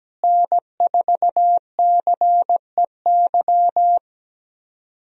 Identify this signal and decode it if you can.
Morse code, heard as a keyed tone, about 17 wpm, 700 Hz. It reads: N4CEY
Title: USA Call Signs - 5 Characters - Repeated 2x - 17wpm